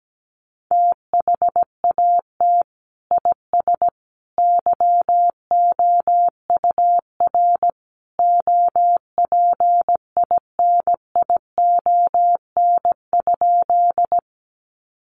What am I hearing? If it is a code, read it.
Morse code, heard as a keyed tone, about 17 wpm, 700 Hz. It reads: THAT IS YOUR OPINION?
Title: Sentences from Top 1200 Words 17wpm